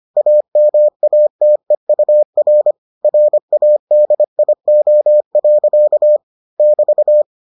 Morse code, heard as a keyed tone, about 25 wpm, 600 Hz. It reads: AMATEUR RADIO. <BT>